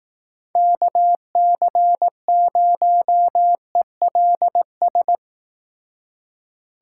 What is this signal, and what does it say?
Morse code, heard as a keyed tone, about 19 wpm, 700 Hz. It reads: KC0ELS
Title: Speed Racing - USA 2x3 Call Signs 15wpm